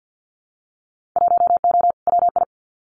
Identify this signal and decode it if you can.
Morse code, heard as a keyed tone, about 50 wpm, 700 Hz. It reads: 1O WI
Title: Winter Field Day Exchanges 50wpm